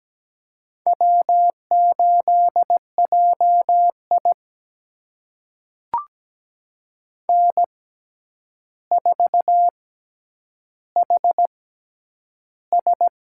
Morse code, heard as a keyed tone, about 17 wpm, spaced as for 9 wpm, 700 Hz. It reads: W8JI N4HS
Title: USA Call Signs - 4 Characters - 6x Character Spacing - 17wpm